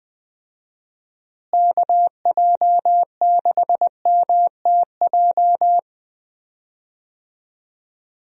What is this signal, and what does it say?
Morse code, heard as a keyed tone, about 21 wpm, 700 Hz. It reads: KJ6MTJ
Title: USA Call Signs - 6 Characters - Repeated 3x - 20wpm